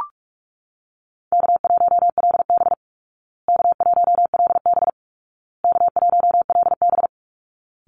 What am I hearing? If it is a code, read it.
Morse code, heard as a keyed tone, about 44 wpm, 700 Hz. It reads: K1LB K1LB K1LB
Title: Easy Call Signs Repeated 3x - 45wpm